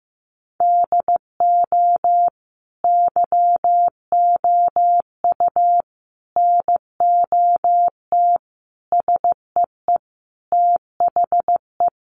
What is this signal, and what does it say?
Morse code, heard as a keyed tone, about 14 wpm, 700 Hz. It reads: DO YOU NOT SEE THE
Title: Sentences from Top 800 Words 15wpm